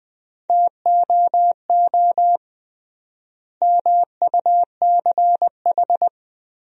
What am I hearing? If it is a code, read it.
Morse code, heard as a keyed tone, about 21 wpm, 700 Hz. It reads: TOO MUCH
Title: 3x Word-Spacing - Sentences from Top 200 Words - 20wpm